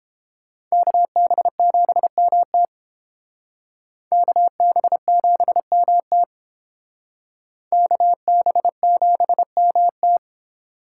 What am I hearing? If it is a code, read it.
Morse code, heard as a keyed tone, about 31 wpm, 700 Hz. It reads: KB7MT KB7MT KB7MT